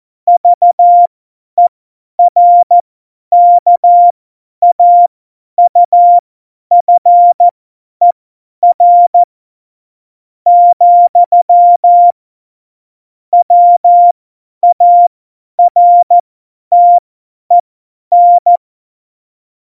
Morse code, heard as a keyed tone, about 14 wpm, spaced as for 10 wpm, 700 Hz. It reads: VERKAUFER , WARTEN
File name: Morsekurs / Witze / Witz 30. Oktober 2018, 50 F70